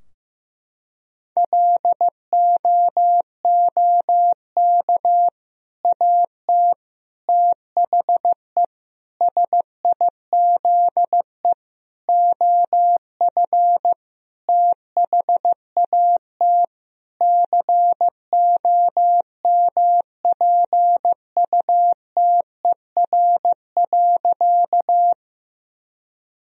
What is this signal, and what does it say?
Morse code, heard as a keyed tone, about 15 wpm, 700 Hz. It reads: LOOK AT THE SIZE OF THAT COMPUTER.
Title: Sentences from Top 1000 Words - Review Part 3 - 15wpm